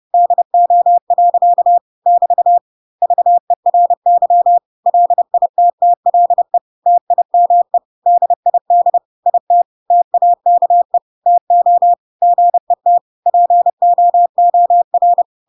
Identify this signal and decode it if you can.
Morse code, heard as a keyed tone, about 30 wpm, 700 Hz. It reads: DO. <BT> VERY LITTLE TIME DID IT TAKE TO GET POOR